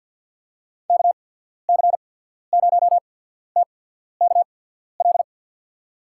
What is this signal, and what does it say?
Morse code, heard as a keyed tone, about 54 wpm, spaced as for 14 wpm, 700 Hz. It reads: KC0TKL
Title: USA Call Signs - 6 Characters - 8x Character Spacing - 50wpm